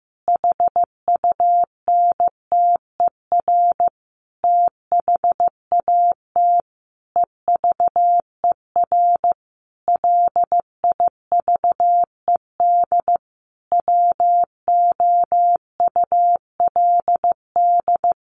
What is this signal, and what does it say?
Morse code, heard as a keyed tone, about 15 wpm, 700 Hz. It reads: HUNTER THAT EVER LIVED WOULD
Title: A Sound Of Thunder 15wpm